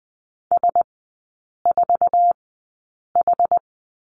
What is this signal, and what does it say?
Morse code, heard as a keyed tone, about 20 wpm, spaced as for 8 wpm, 700 Hz. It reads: S4H
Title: 074 Lesson 13e - Letter L - Three Characters 20wpm